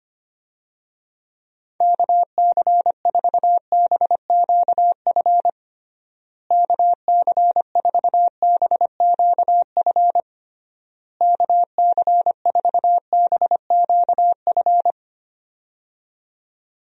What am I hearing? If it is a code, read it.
Morse code, heard as a keyed tone, about 25 wpm, 700 Hz. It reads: KC4BQF KC4BQF KC4BQF
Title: USA Call Signs - 6 Characters - Repeated 3x - 25wpm